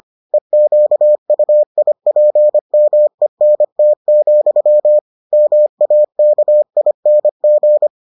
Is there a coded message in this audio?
Yes: Morse code from 0.3 to 7.9 s, heard as a keyed tone, about 25 wpm, 600 Hz. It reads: EQUIPMENT, MAKING